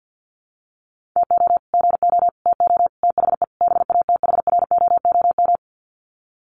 Morse code, heard as a keyed tone, about 48 wpm, 700 Hz. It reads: TO GO TO THE BATHROOM
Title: Sets of 5 Words Encore 50wpm